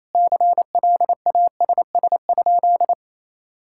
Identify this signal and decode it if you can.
Morse code, heard as a keyed tone, about 28 wpm, 700 Hz. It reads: CLASS?
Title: Sentences from Top 1000 Words - Review Part 6 - 28wpm